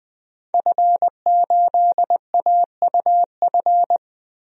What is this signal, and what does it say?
Morse code, heard as a keyed tone, about 20 wpm, 700 Hz. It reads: F8AUF
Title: DXCC-100 Call Signs - 5 Characters - Repeated 2x - 20wpm